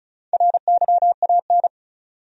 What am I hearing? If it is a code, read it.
Morse code, heard as a keyed tone, about 34 wpm, 700 Hz. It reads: RYAN
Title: USA Common Names - Max 4 Letters - 35wpm